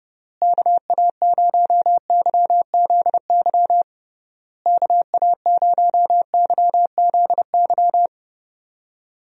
Morse code, heard as a keyed tone, about 30 wpm, 700 Hz. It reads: KA0YZY KA0YZY
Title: USA Call Signs - 6 Characters - Repeated 3x - 30wpm